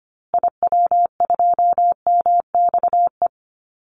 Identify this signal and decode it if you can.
Morse code, heard as a keyed tone, about 25 wpm, 700 Hz. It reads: IW2MXE
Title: Call Signs Repeated 2x - 25wpm